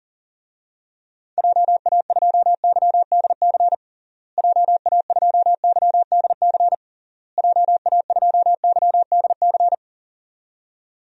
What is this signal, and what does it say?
Morse code, heard as a keyed tone, about 40 wpm, 700 Hz. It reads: JA2YDC JA2YDC JA2YDC
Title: DXCC-100 Call Signs - 6 Characters - Repeated 3x - 40wpm